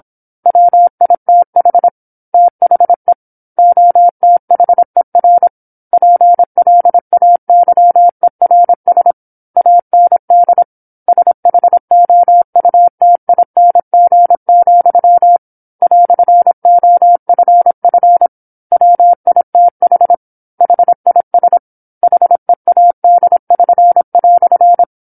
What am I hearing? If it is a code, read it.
Morse code, heard as a keyed tone, about 27 wpm, 700 Hz. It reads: WITH THE OTHER PLAYERS AND SHOUTING, "OFF WITH HIS HEAD<SN>"
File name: 08-Chapter-08 26 WPM